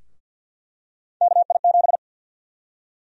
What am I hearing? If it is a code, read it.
Morse code, heard as a keyed tone, about 49 wpm, 700 Hz. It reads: KI7
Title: USA Call Sign Prefixes - Repeated 2x - 50wpm